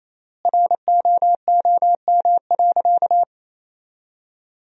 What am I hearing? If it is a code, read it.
Morse code, heard as a keyed tone, about 28 wpm, 700 Hz. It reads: ROOM.
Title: 3x Word-Spacing - Sentences from Top 300 Words - 28wpm